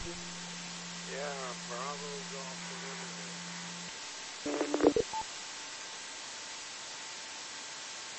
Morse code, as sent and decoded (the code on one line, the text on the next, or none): none